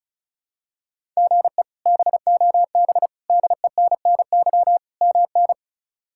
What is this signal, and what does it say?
Morse code, heard as a keyed tone, about 36 wpm, 700 Hz. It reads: GE BOB DENNY MN
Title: K1USN SST QSO Practice - 35wpm